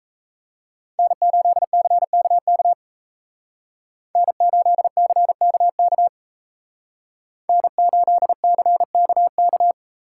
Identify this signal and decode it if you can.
Morse code, heard as a keyed tone, about 34 wpm, 700 Hz. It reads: N8CKK N8CKK N8CKK
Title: Speed Racing - USA Call Signs 28wpm